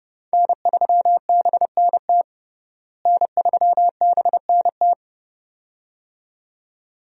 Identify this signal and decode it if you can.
Morse code, heard as a keyed tone, about 30 wpm, 700 Hz. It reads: N3BNT N3BNT
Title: USA Call Signs - 5 Characters - Repeated 2x - 30wpm